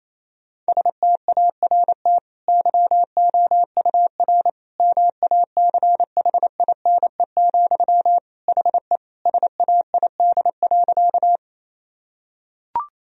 Morse code, heard as a keyed tone, about 28 wpm, 700 Hz. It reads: START YOUR MACHINE, HE SAID.
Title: Sentences from Top 1000 Words - Review Part 2 - 28wpm